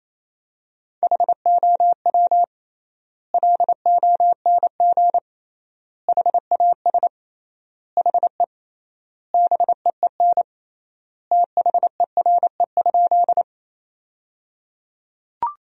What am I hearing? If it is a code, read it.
Morse code, heard as a keyed tone, about 28 wpm, 700 Hz. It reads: HOW LONG HAS HE BEEN THERE?
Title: 3x Word-Spacing - Sentences from Top 100 Words - 28wpm